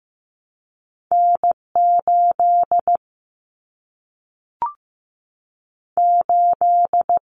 Morse code, heard as a keyed tone, about 15 wpm, 700 Hz. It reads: N 8 8
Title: Common Cut Numbers - 2-Digit 15wpm